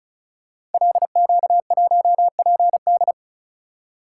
Morse code, heard as a keyed tone, about 36 wpm, 700 Hz. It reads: LQ1PD
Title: Code Groups - 5 Letter:Numbers - 35wpm